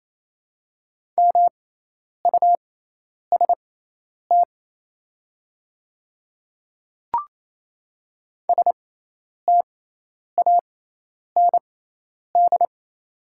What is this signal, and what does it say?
Morse code, heard as a keyed tone, about 27 wpm, spaced as for 10 wpm, 700 Hz. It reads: MUST STAND
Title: 6x Character Spacing - Top 300 Words - 28wpm